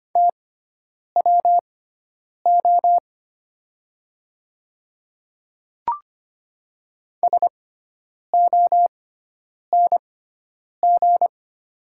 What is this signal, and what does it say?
Morse code, heard as a keyed tone, about 25 wpm, spaced as for 9 wpm, 690 Hz. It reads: TWO SONG
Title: 6x Character Spacing - Top 500 Words - 25wpm